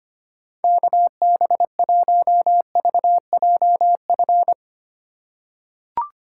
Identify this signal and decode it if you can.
Morse code, heard as a keyed tone, about 25 wpm, 700 Hz. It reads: KB1VJF
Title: USA Call Signs - 6 Characters - 3x Character Spacing - 25wpm